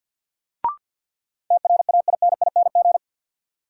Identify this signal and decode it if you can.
Morse code, heard as a keyed tone, about 48 wpm, 700 Hz. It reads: TRAINING
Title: Sentences from Top 1400 Words - New Word Drill Alt - 50wpm